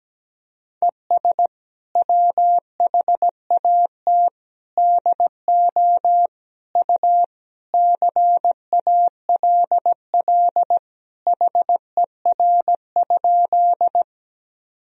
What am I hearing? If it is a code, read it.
Morse code, heard as a keyed tone, about 17 wpm, 700 Hz. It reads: ES WHAT DO U CALL HER?
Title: Sentences from Top 100 Words and Abbreviations 17wpm